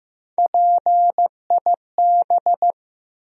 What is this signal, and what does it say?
Morse code, heard as a keyed tone, about 15 wpm, 700 Hz. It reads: PIB
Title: USA Call Sign Postfixes - 3 Letters 15wpm